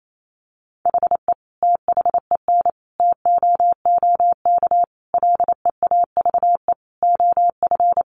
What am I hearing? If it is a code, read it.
Morse code, heard as a keyed tone, about 28 wpm, 700 Hz. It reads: HE THEN TOOK LEAVE OF
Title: Sentences from Top 1400 Words 28wpm